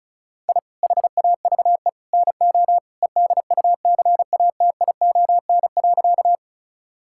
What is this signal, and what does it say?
Morse code, heard as a keyed tone, about 36 wpm, 700 Hz. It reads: I HAVE NO EDUCATION.
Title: Sentences from Top 1000 Words - Review Part 6 - 35wpm